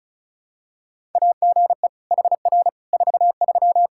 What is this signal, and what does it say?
Morse code, heard as a keyed tone, about 36 wpm, 700 Hz. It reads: AGE HR 43
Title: Practice QSO - Exchange 3 - More Info - 35wpm